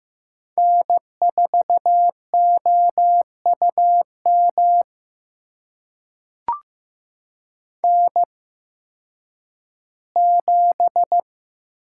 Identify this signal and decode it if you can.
Morse code, heard as a keyed tone, about 15 wpm, spaced as for 10 wpm, 700 Hz. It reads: N4OUM N7
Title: USA Call Signs - 5 Characters - 8x Character Spacing - 15wpm